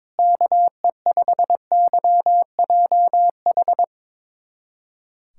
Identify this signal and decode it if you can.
Morse code, heard as a keyed tone, about 22 wpm, 700 Hz. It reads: KE5YJH
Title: USA Callsign DE Callsign 22wpm